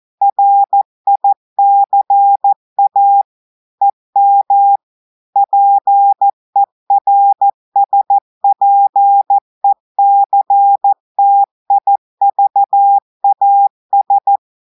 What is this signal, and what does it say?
Morse code, heard as a keyed tone, about 14 wpm, 800 Hz. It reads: RICA EM PERSPECTIVAS
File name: audios / portugais / portugais texte en clair 1 vitesse 14 mots minutes